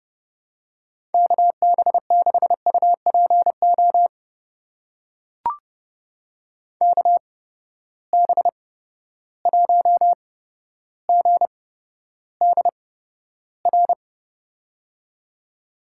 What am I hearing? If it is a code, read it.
Morse code, heard as a keyed tone, about 30 wpm, spaced as for 13 wpm, 700 Hz. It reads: KB6UPO KB1GDR
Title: USA Call Signs - 6 Characters - 8x Character Spacing - 30wpm